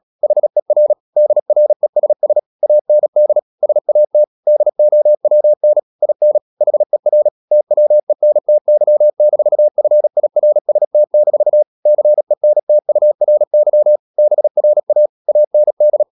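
Morse code, heard as a keyed tone, about 37 wpm, 600 Hz. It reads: HER DRESS AND SAT DOWN IN HER TWENTY-FIRST-CENTURY BRA AND
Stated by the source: Morse practice at 36 wpm, text of Ken Follett's novel Never, from CW med Gnister.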